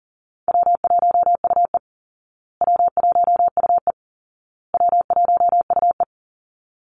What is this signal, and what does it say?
Morse code, heard as a keyed tone, about 40 wpm, 700 Hz. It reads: W1UE W1UE W1UE
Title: Easy Call Signs Repeated 3x - 40wpm